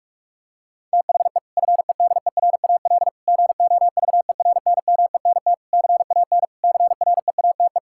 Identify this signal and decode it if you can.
Morse code, heard as a keyed tone, about 46 wpm, 700 Hz. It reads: THE FEDERAL GOVERNMENT CAN CREATE
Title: Sentences from Top 1600 Words 45wpm